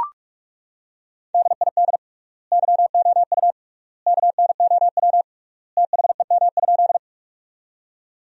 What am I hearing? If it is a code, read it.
Morse code, heard as a keyed tone, about 43 wpm, 690 Hz. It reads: DID YOU KNOW THEM?
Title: 3x Word-Spacing - Sentences from Top 100 Words - 45wpm